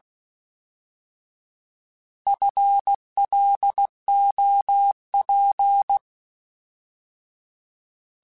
Morse code, heard as a keyed tone, about 16 wpm, 800 Hz. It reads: FLOP
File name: Alphabetical / Alphabetical mp3s / Fs @15